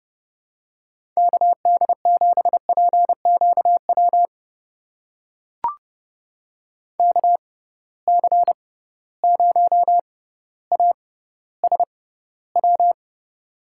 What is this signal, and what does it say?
Morse code, heard as a keyed tone, about 30 wpm, spaced as for 15 wpm, 700 Hz. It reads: KD7PQW KC0ASW
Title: USA Call Signs - 6 Characters - 6x Character Spacing - 30wpm